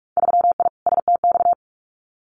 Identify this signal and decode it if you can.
Morse code, heard as a keyed tone, about 46 wpm, 700 Hz. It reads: 3I STX
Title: Winter Field Day Exchanges 45wpm